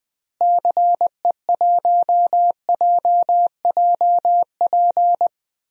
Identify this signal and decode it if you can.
Morse code, heard as a keyed tone, about 21 wpm, 700 Hz. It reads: CE1JJP
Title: DXCC-100 Call Signs - 6 Characters - Repeated 2x - 20wpm